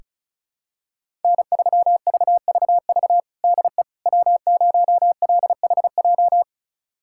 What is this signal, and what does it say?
Morse code, heard as a keyed tone, about 36 wpm, 700 Hz. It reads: N3VVV DE W0LHJ
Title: USA Callsign DE Callsign - 5 Characters - 35wpm